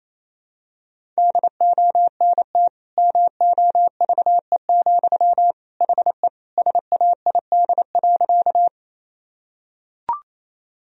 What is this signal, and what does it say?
Morse code, heard as a keyed tone, about 28 wpm, 700 Hz. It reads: DONT MOVE, HE SAID.
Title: Sentences from Top 1000 Words - Review Part 1 - 28wpm